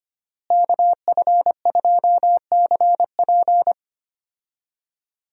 Morse code, heard as a keyed tone, about 25 wpm, 700 Hz. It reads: KF2CP
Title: USA Call Signs - 5 Characters - Repeated 3x - 25wpm